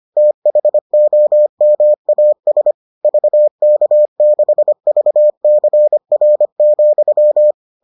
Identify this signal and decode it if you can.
Morse code, heard as a keyed tone, about 25 wpm, 600 Hz. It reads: THOMAS VK6VCR,